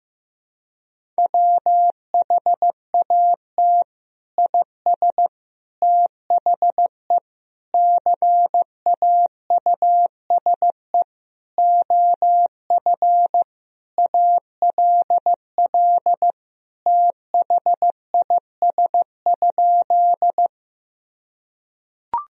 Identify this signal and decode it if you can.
Morse code, heard as a keyed tone, about 15 wpm, 700 Hz. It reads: WHAT IS THE CAUSE OF ALL THIS?
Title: Sentences from Top 1000 Words - Review Part 1 - 15wpm